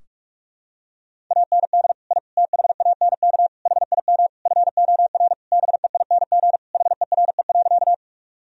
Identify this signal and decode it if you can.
Morse code, heard as a keyed tone, about 46 wpm, 700 Hz. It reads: AND I THANK HIM FOR BEING HERE.
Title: Sentences from Top 1000 Words - Review Part 3 - 45wpm